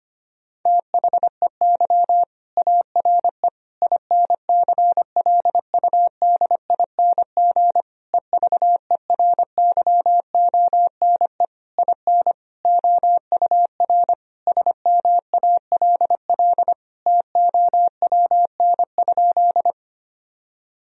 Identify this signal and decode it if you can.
Morse code, heard as a keyed tone, about 25 wpm, 700 Hz. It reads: THEY ARE INCLUDING EVERYONE IN OUR SMALL TOWN?
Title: Sentences from Top 1000 Words - Review Part 4 - 25wpm